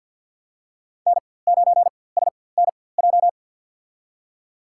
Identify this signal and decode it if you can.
Morse code, heard as a keyed tone, about 50 wpm, spaced as for 23 wpm, 700 Hz. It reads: N9SNJ
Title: USA Call Signs - 4x Character Spacing - 50wpm